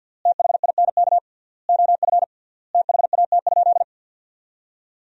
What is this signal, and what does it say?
Morse code, heard as a keyed tone, about 51 wpm, 700 Hz. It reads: THINK OF THAT?
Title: 3x Word-Spacing - Sentences from Top 200 Words - 50wpm